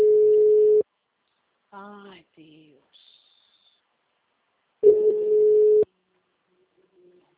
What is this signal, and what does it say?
Ringback tone (Europe, 425 Hz), with speech.